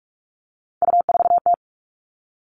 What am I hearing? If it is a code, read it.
Morse code, heard as a keyed tone, about 44 wpm, 700 Hz. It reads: U4T
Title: All Cut Numbers - 3-Digit 45wpm